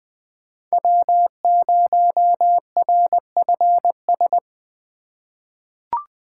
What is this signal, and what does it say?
Morse code, heard as a keyed tone, about 20 wpm, 700 Hz. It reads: W0RFS
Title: USA Call Signs - 5 Characters - 4x Character Spacing - 20wpm